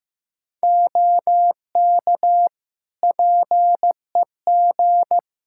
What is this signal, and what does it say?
Morse code, heard as a keyed tone, about 16 wpm, 700 Hz. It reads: OK PEG
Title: Practice QSO - Exchange 2 - Info - 15wpm